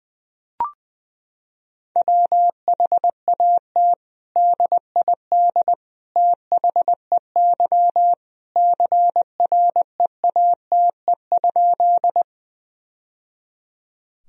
Morse code, heard as a keyed tone, about 19 wpm, 700 Hz. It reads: WHAT DID THEY CREATE?
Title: Sentences from Top 1000 Words - Review Part 3 - 20wpm